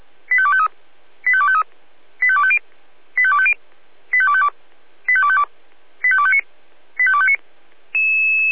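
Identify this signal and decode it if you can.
Five-tone selective call (ZVEI), heard as quick runs of five tones, tones of about 70 ms. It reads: 85253 85253 85250 85250 85252 85252 85259 85259
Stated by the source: Five-tone ZVEI selective call 85358, received by SDR.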